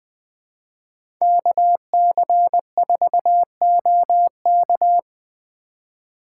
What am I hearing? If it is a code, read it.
Morse code, heard as a keyed tone, about 20 wpm, 700 Hz. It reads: KC4OK
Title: USA Call Signs - 5 Characters - 2x Character Spacing - 20wpm